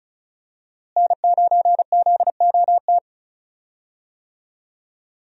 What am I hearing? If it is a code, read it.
Morse code, heard as a keyed tone, about 36 wpm, 700 Hz. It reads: N9ZOT
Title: USA Call Signs - 5 Characters - Repeated 2x - 35wpm